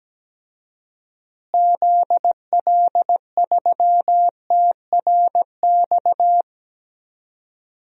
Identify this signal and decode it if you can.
Morse code, heard as a keyed tone, about 17 wpm, 700 Hz. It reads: ZL3TRX
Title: DXCC-100 Call Signs - 6 Characters - Repeated 3x - 17wpm